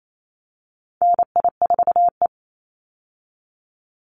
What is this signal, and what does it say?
Morse code, heard as a keyed tone, about 28 wpm, 700 Hz. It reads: NI4E
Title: Call Signs - 28wpm